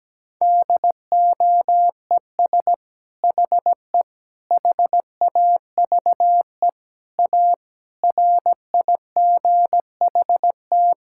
Morse code, heard as a keyed tone, about 17 wpm, 700 Hz. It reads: DOES HE HAVE A RIGHT
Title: Sentences from Top 1000 Words - Review Part 5 - 17wpm